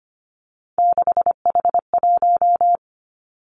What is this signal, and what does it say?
Morse code, heard as a keyed tone, about 25 wpm, 700 Hz. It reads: 6H1
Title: Three Challenging Letter-Number Combinations 25wpm